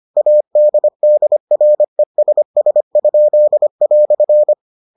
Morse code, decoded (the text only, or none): ADDRESS?"